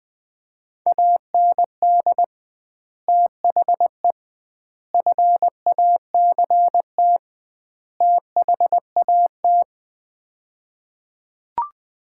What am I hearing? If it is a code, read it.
Morse code, heard as a keyed tone, about 21 wpm, 700 Hz. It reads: AND THE FACT THAT
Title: 2x Word-Spacing - Sets of 4 Words - 20wpm